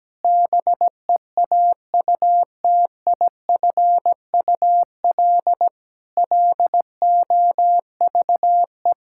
Morse code, heard as a keyed tone, about 17 wpm, 700 Hz. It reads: BEAUTIFUL LOVE
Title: Sentences from Top 1600 Words 17wpm